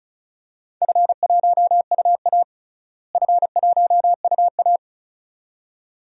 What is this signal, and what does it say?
Morse code, heard as a keyed tone, about 34 wpm, 700 Hz. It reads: F1UA F1UA
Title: DXCC-100 Call Signs - 4 Characters - Repeated 2x - 35wpm